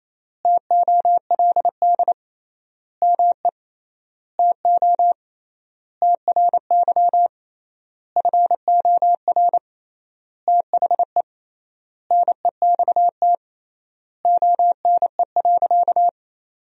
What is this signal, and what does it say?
Morse code, heard as a keyed tone, about 28 wpm, 700 Hz. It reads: TOLD ME TO TRY FOR THE NEXT ONE.
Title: 3x Word-Spacing - Sentences from Top 400 Words - 28wpm